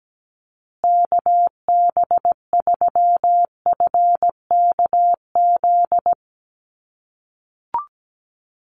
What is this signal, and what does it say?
Morse code, heard as a keyed tone, about 17 wpm, 700 Hz. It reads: KB3FKZ
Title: USA Call Signs Repeated 3x 17wpm